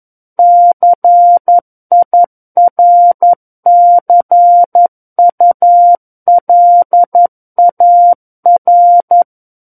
Morse code, heard as a keyed tone, about 11 wpm, 700 Hz. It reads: CIRCULAR
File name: 11 WPM